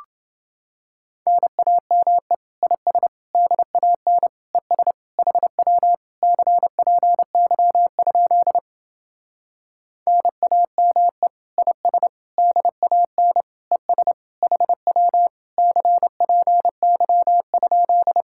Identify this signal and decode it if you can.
Morse code, heard as a keyed tone, about 28 wpm, 700 Hz. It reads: NAME IS DAN ES HW CPY? NAME IS DAN ES HW CPY?